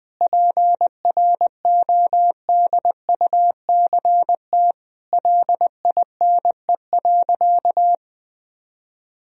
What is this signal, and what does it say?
Morse code, heard as a keyed tone, about 20 wpm, 700 Hz. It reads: PRODUCT LINE.
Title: Sentences from Top 1000 Words - Review Part 3 - 20wpm